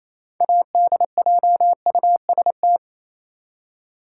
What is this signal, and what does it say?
Morse code, heard as a keyed tone, about 28 wpm, 700 Hz. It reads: ADJUST
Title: Six Letter Words - 28wpm